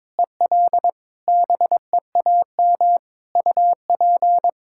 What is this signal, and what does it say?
Morse code, heard as a keyed tone, about 22 wpm, 700 Hz. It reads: EL BEAM UP
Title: Practice QSO - Exchange 2 - Info - 22wpm